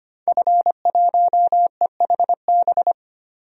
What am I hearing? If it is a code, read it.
Morse code, heard as a keyed tone, about 25 wpm, 700 Hz. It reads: F1EHB
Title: DXCC-100 Call Signs - 5 Characters - 25wpm